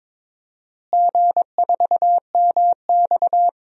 Morse code, heard as a keyed tone, about 22 wpm, 700 Hz. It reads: G4MX
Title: DXCC-100 Call Signs - 4 Characters - Repeated 3x - 22wpm